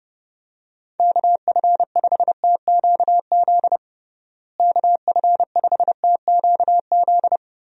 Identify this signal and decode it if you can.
Morse code, heard as a keyed tone, about 32 wpm, 700 Hz. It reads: KF5TQZ KF5TQZ
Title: USA Call Signs - 6 Characters - Repeated 2x - 30wpm